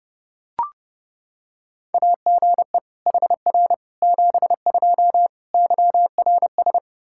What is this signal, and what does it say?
Morse code, heard as a keyed tone, about 32 wpm, 700 Hz. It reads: AGE HR 72 YRS
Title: Practice QSO - Exchange 3 - More Info - 30wpm